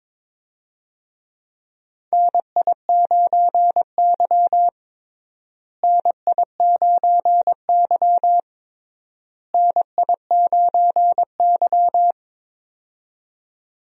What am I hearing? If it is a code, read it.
Morse code, heard as a keyed tone, about 22 wpm, 700 Hz. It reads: NI9Y NI9Y NI9Y
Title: USA Call Signs - 4 Characters - Repeated 3x - 22wpm